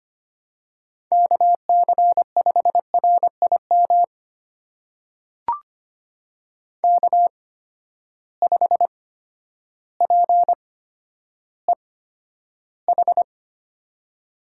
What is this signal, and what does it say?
Morse code, heard as a keyed tone, about 25 wpm, spaced as for 11 wpm, 700 Hz. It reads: KC5RIM K5PEH
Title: USA Call Signs - 8x Character Spacing - 25wpm